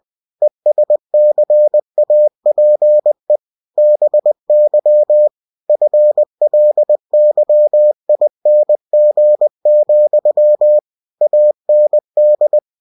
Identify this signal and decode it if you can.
Morse code, heard as a keyed tone, about 20 wpm, 600 Hz. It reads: ESCAPE BY FLYING, AND
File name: cw-books / winnie-the-pooh / winnie-20wpm-page-0091